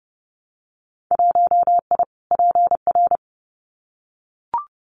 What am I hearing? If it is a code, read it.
Morse code, heard as a keyed tone, about 30 wpm, 700 Hz. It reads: 1I PR
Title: Winter Field Day Exchanges Encore 30wpm